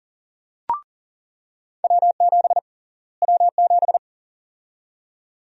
Morse code, heard as a keyed tone, about 39 wpm, 700 Hz. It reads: W7 W7
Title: USA Call Sign Prefixes - Repeated 2x - 40wpm